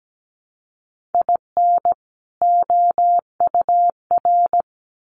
Morse code, heard as a keyed tone, about 17 wpm, 700 Hz. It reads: IN OUR
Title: Sets of 2 Words 17wpm